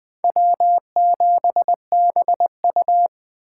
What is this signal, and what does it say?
Morse code, heard as a keyed tone, about 20 wpm, 700 Hz. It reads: W7BU
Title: Speed Racing - USA 1x2 Call Signs 17wpm